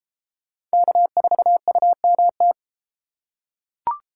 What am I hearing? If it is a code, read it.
Morse code, heard as a keyed tone, about 31 wpm, 700 Hz. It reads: K4UMT